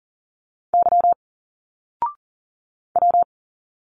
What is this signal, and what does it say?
Morse code, heard as a keyed tone, about 40 wpm, 700 Hz. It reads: Y W
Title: Single Letters 40wpm